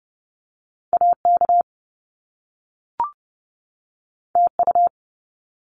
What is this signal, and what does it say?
Morse code, heard as a keyed tone, about 30 wpm, 700 Hz. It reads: AK TU
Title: Top Two Letter Combinations in Words 30wpm